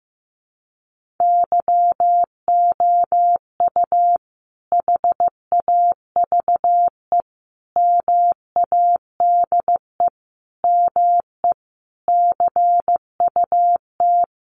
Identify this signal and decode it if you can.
Morse code, heard as a keyed tone, about 15 wpm, 700 Hz. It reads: YOU HAVE MADE ME CUT
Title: Sentences from Top 1300 Words 15wpm